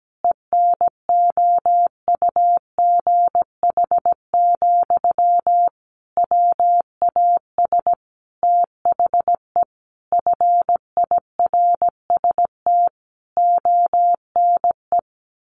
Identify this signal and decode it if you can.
Morse code, heard as a keyed tone, about 17 wpm, 700 Hz. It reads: ENOUGH, WAS THE FIRST ONE
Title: Sentences from Top 1300 Words 17wpm